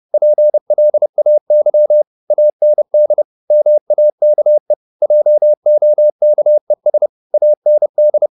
Morse code, heard as a keyed tone, about 30 wpm, 600 Hz. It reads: PLAY AND MAKE JOKES AND
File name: the old man and the sea / the old man and the sea 1 30wpm